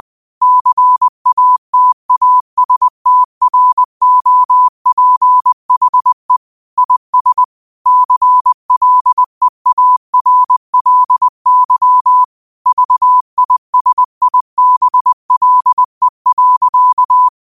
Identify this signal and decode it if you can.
Morse code, heard as a keyed tone, about 20 wpm, 1000 Hz. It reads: CATASTROPHE IS CLEARLY VISIBLE.
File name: qotdfiles / 20wpm